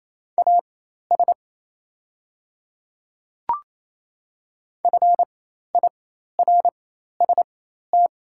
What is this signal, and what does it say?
Morse code, heard as a keyed tone, about 28 wpm, spaced as for 13 wpm, 700 Hz. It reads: AS FIRST
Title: 4x Character Spacing - Top 100 Words - 28wpm